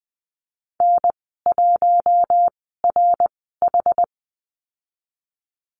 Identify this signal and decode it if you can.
Morse code, heard as a keyed tone, about 20 wpm, spaced as for 15 wpm, 700 Hz. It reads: N1RH
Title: 069 Lesson 12g - Letter d - Call Signs 20wpm